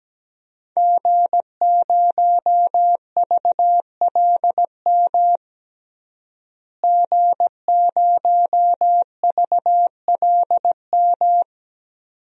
Morse code, heard as a keyed tone, about 17 wpm, 700 Hz. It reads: G0VLM G0VLM
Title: DXCC-100 Call Signs - 5 Characters - Repeated 2x - 17wpm